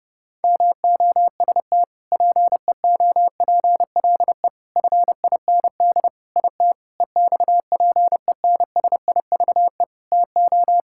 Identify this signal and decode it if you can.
Morse code, heard as a keyed tone, about 30 wpm, 700 Hz. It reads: MOST PEOPLE FIND IT EXPENSIVE TO
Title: Sentences from Top 1000 Words - Review Part 6 - 30wpm